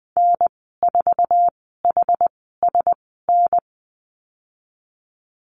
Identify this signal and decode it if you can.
Morse code, heard as a keyed tone, about 20 wpm, spaced as for 14 wpm, 700 Hz. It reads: N4HSN
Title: 082 Lesson 14g - Number 2 - Call Signs 20wpm